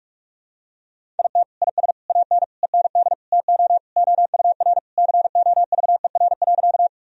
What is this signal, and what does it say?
Morse code, heard as a keyed tone, about 46 wpm, 700 Hz. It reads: IT IS AN END TO OUR COVER.
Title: Sentences from Top 1000 Words - Review Part 1 - 45wpm